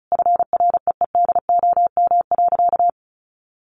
Morse code, heard as a keyed tone, about 34 wpm, 700 Hz. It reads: FREEDOM.
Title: Sentences from Top 1100 Words 35wpm